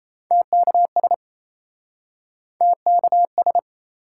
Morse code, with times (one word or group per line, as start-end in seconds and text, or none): 0.3-1.1 TKS
2.6-3.6 TKS